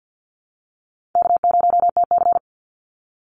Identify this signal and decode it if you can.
Morse code, heard as a keyed tone, about 50 wpm, 700 Hz. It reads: K0TC
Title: Easy Call Signs - 50wpm